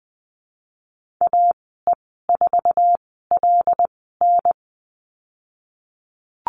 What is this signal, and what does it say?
Morse code, heard as a keyed tone, about 20 wpm, spaced as for 15 wpm, 700 Hz. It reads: AE4LN
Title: 088 Lesson 15g - Number 5 - Call Signs 20wpm